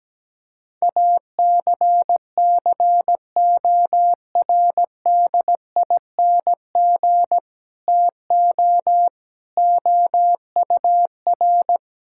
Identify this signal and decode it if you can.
Morse code, heard as a keyed tone, about 17 wpm, 700 Hz. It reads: ACCORDING TO OUR
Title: Sentences from Top 1000 Words - Review Part 5 - 17wpm